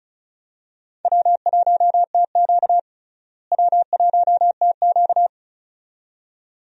Morse code, heard as a keyed tone, about 36 wpm, 700 Hz. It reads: W1TQ W1TQ
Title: USA Call Signs - 4 Characters - Repeated 2x - 35wpm